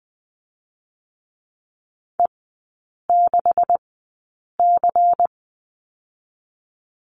Morse code, heard as a keyed tone, about 20 wpm, spaced as for 8 wpm, 700 Hz. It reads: E6C
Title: 127 Lesson 21e - Number 6 - Three Characters 20wpm